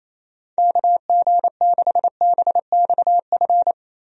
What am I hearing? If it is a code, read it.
Morse code, heard as a keyed tone, about 28 wpm, 700 Hz. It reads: KG6BXF
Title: USA Call Signs - 6 Characters - Repeated 3x - 28wpm